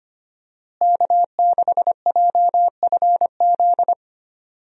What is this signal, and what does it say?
Morse code, heard as a keyed tone, about 25 wpm, 700 Hz. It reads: K6JFZ
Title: USA Call Signs - 5 Characters - Repeated 2x - 25wpm